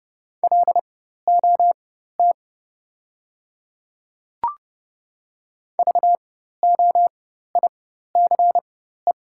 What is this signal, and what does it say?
Morse code, heard as a keyed tone, about 30 wpm, spaced as for 14 wpm, 690 Hz. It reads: LOT VOICE
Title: 4x Character Spacing - Top 500 Words - 30wpm